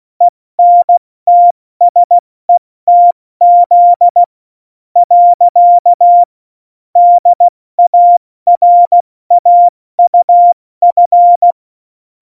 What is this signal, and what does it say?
Morse code, heard as a keyed tone, about 16 wpm, 700 Hz. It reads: ENTSETZ . DARAUF
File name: Morsekurs / Witze / Witz 20 March 2018, 70 F80